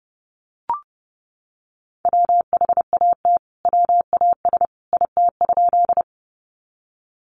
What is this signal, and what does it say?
Morse code, heard as a keyed tone, about 30 wpm, 700 Hz. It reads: WHAT WAS IT?
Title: Sentences from Top 100 Words 30wpm